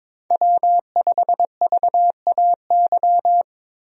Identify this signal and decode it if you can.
Morse code, heard as a keyed tone, about 22 wpm, 700 Hz. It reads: W5VAY
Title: USA Call Signs - 5 Characters - Repeated 2x - 22wpm